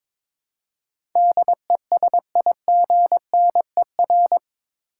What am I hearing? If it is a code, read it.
Morse code, heard as a keyed tone, about 21 wpm, 700 Hz. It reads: DESIGNER
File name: practice / 22wpm / Eight Letter Words - 22wpm